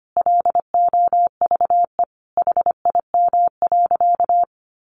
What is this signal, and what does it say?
Morse code, heard as a keyed tone, about 25 wpm, 700 Hz. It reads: LOVE HIM.
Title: Sentences from Top 1100 Words 25wpm